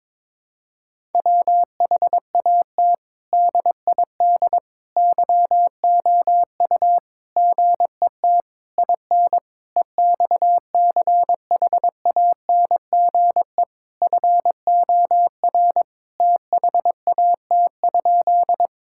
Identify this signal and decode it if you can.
Morse code, heard as a keyed tone, about 22 wpm, 700 Hz. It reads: WHAT DID YOU GET IN EXCHANGE FOR THAT?
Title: Sentences from Top 1000 Words - Review Part 6 - 22wpm